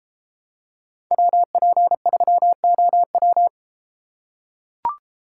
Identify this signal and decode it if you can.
Morse code, heard as a keyed tone, about 32 wpm, 700 Hz. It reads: WP3OW